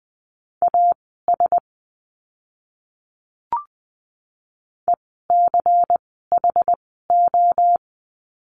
Morse code, heard as a keyed tone, about 20 wpm, spaced as for 14 wpm, 700 Hz. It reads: AS ECHO
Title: 094 Lesson 16f - Letter c - Words 20wpm